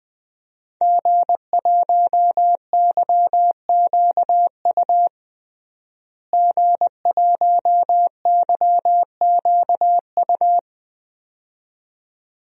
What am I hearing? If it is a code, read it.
Morse code, heard as a keyed tone, about 20 wpm, 700 Hz. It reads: G1YQU G1YQU
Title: DXCC-100 Call Signs - 5 Characters - Repeated 2x - 20wpm